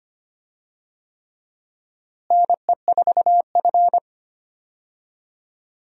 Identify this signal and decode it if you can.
Morse code, heard as a keyed tone, about 25 wpm, 700 Hz. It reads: NE4F
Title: USA Call Signs - 4 Characters - Repeated 3x - 25wpm